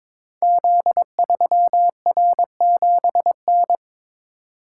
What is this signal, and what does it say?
Morse code, heard as a keyed tone, about 22 wpm, 700 Hz. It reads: Z3R7N
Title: Code Groups - 5 Letter:Numbers - 22wpm